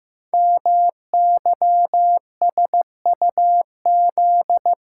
Morse code, heard as a keyed tone, about 15 wpm, 700 Hz. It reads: MYSUZ
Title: Code Groups - 5 Letters - 15wpm